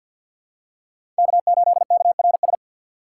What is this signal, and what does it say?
Morse code, heard as a keyed tone, about 52 wpm, 700 Hz. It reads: K8KRS
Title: USA Call Signs - 5 Characters - 2x Character Spacing - 50wpm